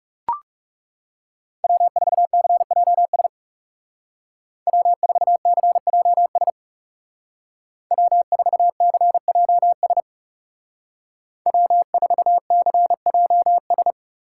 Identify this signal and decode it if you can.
Morse code, heard as a keyed tone, about 36 wpm, 700 Hz. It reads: W4CJS W4CJS W4CJS W4CJS
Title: Speed Racing - USA 1x3 Call Signs 30wpm